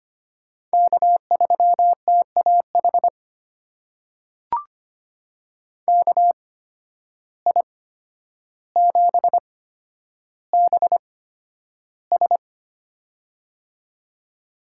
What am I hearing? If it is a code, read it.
Morse code, heard as a keyed tone, about 25 wpm, spaced as for 11 wpm, 700 Hz. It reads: K3TAH KI7BS
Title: USA Call Signs - 5 Characters - 8x Character Spacing - 25wpm